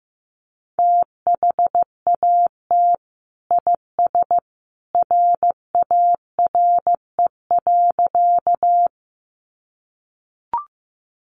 Morse code, heard as a keyed tone, about 15 wpm, 700 Hz. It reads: THAT IS RARE.
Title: Sentences from Top 1100 Words 15wpm